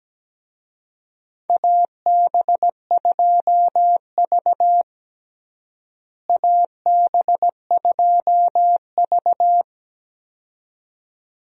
Morse code, heard as a keyed tone, about 17 wpm, 700 Hz. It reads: AB2V AB2V
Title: USA Call Signs - 4 Characters - Repeated 2x - 17wpm